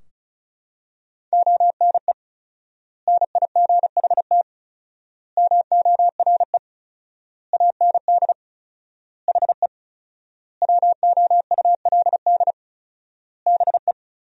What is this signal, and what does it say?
Morse code, heard as a keyed tone, about 34 wpm, 700 Hz. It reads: ONE NIGHT MORE AND HE WOULD BE
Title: 4x Word-Spacing - Sentences from Top 300 Words - 35wpm